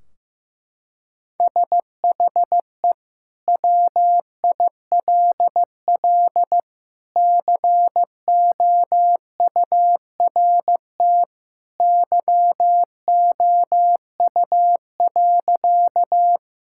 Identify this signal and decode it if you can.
Morse code, heard as a keyed tone, about 15 wpm, 700 Hz. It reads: SHE WILL COURT YOU.
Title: Sentences from Top 1600 Words 15wpm